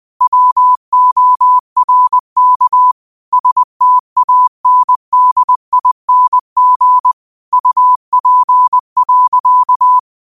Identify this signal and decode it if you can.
Morse code, heard as a keyed tone, about 20 wpm, 1000 Hz. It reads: WORK STANDING UP.
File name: qotdfiles / 20wpm